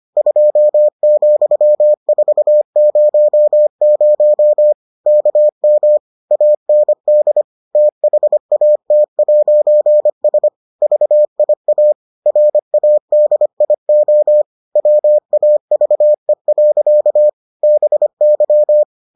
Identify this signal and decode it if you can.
Morse code, heard as a keyed tone, about 25 wpm, 600 Hz. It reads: 2,400 KM AND THAT'S VIA RADIO WAVE. BY